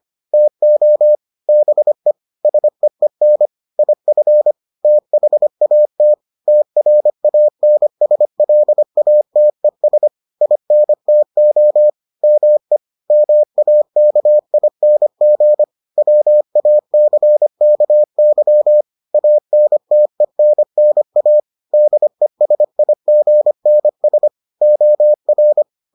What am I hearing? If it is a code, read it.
Morse code, heard as a keyed tone, about 25 wpm, 600 Hz. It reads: TO BE SEEN IF THAT TRANSLATES INTO ME MAKING WACKY ANTENNA DESIGNS OR